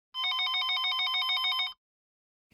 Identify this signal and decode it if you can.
There is no code.